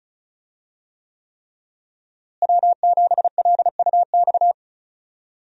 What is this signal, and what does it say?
Morse code, heard as a keyed tone, about 35 wpm, 700 Hz. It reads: W7LUX
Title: USA Call Signs - 5 Characters - Repeated 3x - 35wpm